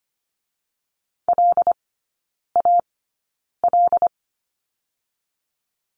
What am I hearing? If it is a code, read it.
Morse code, heard as a keyed tone, about 25 wpm, spaced as for 9 wpm, 690 Hz. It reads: LAL
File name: Lesson 14e - Number 2 - Three Characters 25wpm